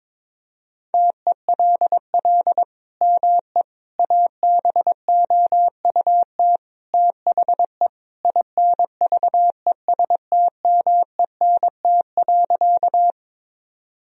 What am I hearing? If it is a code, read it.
Morse code, heard as a keyed tone, about 22 wpm, 700 Hz. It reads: TELL ME ABOUT THE INVESTMENT.
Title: Sentences from Top 1000 Words - Review Part 4 - 22wpm